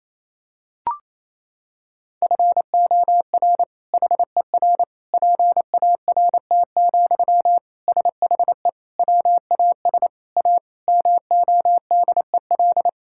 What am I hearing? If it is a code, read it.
Morse code, heard as a keyed tone, about 28 wpm, 700 Hz. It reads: FOR HER PART, SHE WAS A MODEL
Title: Sentences from Top 1000 Words - Review Part 5 - 28wpm